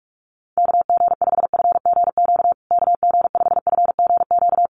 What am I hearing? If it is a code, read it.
Morse code, heard as a keyed tone, about 44 wpm, 690 Hz. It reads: KG5FGQ KG5FGQ
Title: USA Call Signs Repeated 3x 45wpm